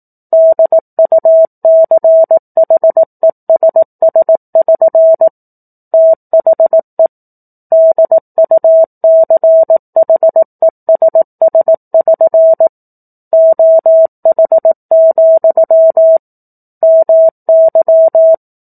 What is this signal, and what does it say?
Morse code, heard as a keyed tone, about 18 wpm, 650 Hz. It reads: DUCHESS<SN> THE DUCHESS<SN> OH, MY
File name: 04-Chapter-04 18 WPM